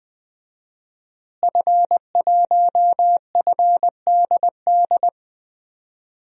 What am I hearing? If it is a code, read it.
Morse code, heard as a keyed tone, about 20 wpm, 700 Hz. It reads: F1FDD
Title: DXCC-100 Call Signs - 5 Characters - Repeated 2x - 20wpm